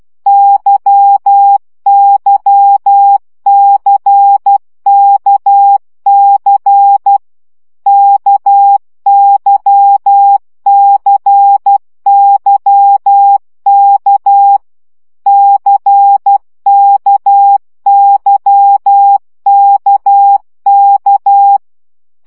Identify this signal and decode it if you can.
Morse code, heard as a keyed tone, about 12 wpm, 800 Hz. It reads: YYCKC KYCYK CKYKK
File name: CW gyakorlas k c y 10 baud